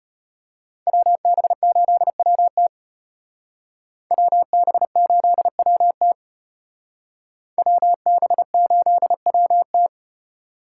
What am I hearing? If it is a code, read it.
Morse code, heard as a keyed tone, about 29 wpm, 700 Hz. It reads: WB8WT WB8WT WB8WT
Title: Speed Racing - USA 2x2 Call Signs 25wpm